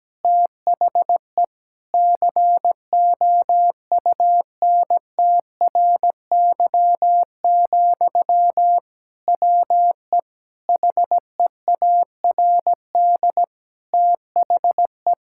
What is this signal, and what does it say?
Morse code, heard as a keyed tone, about 17 wpm, 700 Hz. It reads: THE COUNTRY, WE HEARD THE
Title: Sentences from Top 1000 Words - Review Part 5 - 17wpm